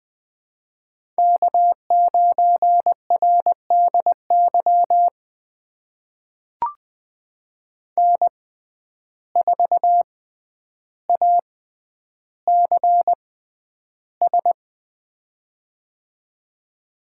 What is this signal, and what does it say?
Morse code, heard as a keyed tone, about 20 wpm, spaced as for 10 wpm, 700 Hz. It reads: K9RDY N4ACS
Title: USA Call Signs - 5 Characters - 6x Character Spacing - 20wpm